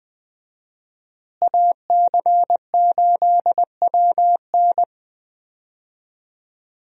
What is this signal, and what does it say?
Morse code, heard as a keyed tone, about 20 wpm, 700 Hz. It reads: AC8WN
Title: USA Call Signs - 5 Characters - Repeated 2x - 20wpm